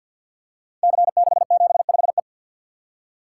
Morse code, heard as a keyed tone, about 53 wpm, 700 Hz. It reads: C675E
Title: Code Groups - 5 Letter:Numbers - 50wpm